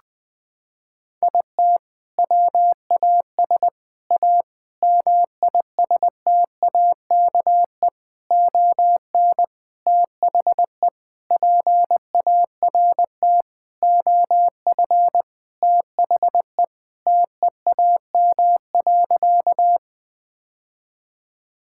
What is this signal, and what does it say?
Morse code, heard as a keyed tone, about 20 wpm, 700 Hz. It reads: IT WAS A MISTAKE ON THE PART OF THE TEAM.
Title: Sentences with 4-Word Phrases - 20wpm